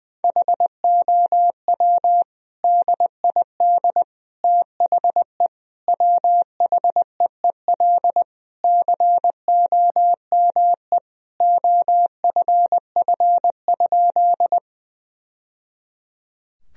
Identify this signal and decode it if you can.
Morse code, heard as a keyed tone, about 20 wpm, 700 Hz. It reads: HOW DID THE WHEEL COME OFF?
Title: Sentences from Top 1000 Words - Review Part 3 - 20wpm